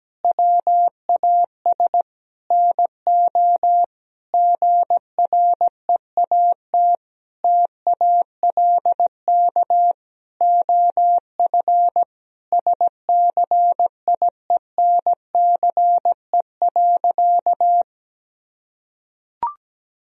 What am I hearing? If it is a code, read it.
Morse code, heard as a keyed tone, about 17 wpm, 700 Hz. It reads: WAS NO GREAT TALK OF SCIENCE.
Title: Sentences from Top 1000 Words - Review Part 2 - 17wpm